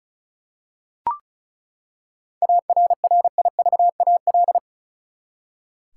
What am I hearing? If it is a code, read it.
Morse code, heard as a keyed tone, about 36 wpm, 700 Hz. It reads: ARRIVAL
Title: Seven Letter Words - 35wpm